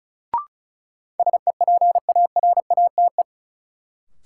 Morse code, heard as a keyed tone, about 36 wpm, 700 Hz. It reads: SEPARATE
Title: Sentences from Top 1000 Words - Review Part 7 - New Word Drill - 35wpm